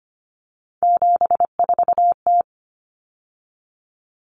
Morse code, heard as a keyed tone, about 25 wpm, 700 Hz. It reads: 74T
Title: Common Cut Numbers - 3-Digit 25wpm